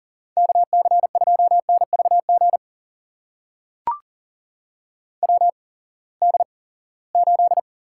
Morse code, heard as a keyed tone, about 40 wpm, spaced as for 23 wpm, 700 Hz. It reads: KC2NVG WD8
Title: USA Call Signs - 6 Characters - 8x Character Spacing - 40wpm